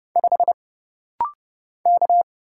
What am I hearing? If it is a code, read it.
Morse code, heard as a keyed tone, about 30 wpm, 700 Hz. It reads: 5 K
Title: Single Letter-Number - Rapid-Fire 30wpm